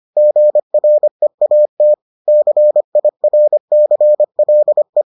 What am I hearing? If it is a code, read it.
Morse code, heard as a keyed tone, about 25 wpm, 600 Hz. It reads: GREAT CIRCLE